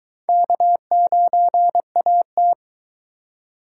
Morse code, heard as a keyed tone, about 22 wpm, 700 Hz. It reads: K9AT